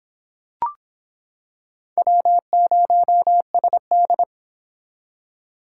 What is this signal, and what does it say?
Morse code, heard as a keyed tone, about 25 wpm, 700 Hz. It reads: W0SD